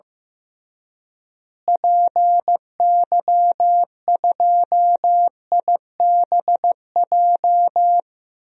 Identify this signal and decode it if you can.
Morse code, heard as a keyed tone, about 15 wpm, 700 Hz. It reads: PY2IBJ
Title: DXCC-100 Call Signs - 6 Characters - Repeated 3x - 15wpm